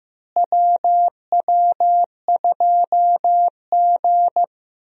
Morse code, heard as a keyed tone, about 16 wpm, 700 Hz. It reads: WW2G
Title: USA Call Signs - 4 Characters - Repeated 2x - 15wpm